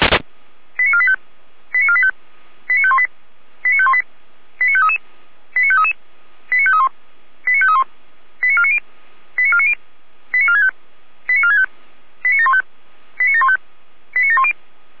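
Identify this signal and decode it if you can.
Five-tone selective call (ZVEI), heard as quick runs of five tones, tones of about 70 ms. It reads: 89485 89485 89518 89518 89533 89533 89531 89531 89490 89490 89465 89465 89715 89715 89710
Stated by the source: ZVEI five-tone selective call received on an SDR.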